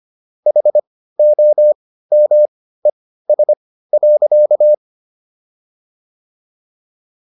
Morse code, heard as a keyed tone, about 25 wpm, spaced as for 15 wpm, 600 Hz. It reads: HOMES.